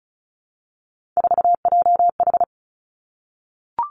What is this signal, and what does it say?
Morse code, heard as a keyed tone, about 36 wpm, 700 Hz. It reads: 4JH
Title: Three Challenging Letter-Number Combinations 35wpm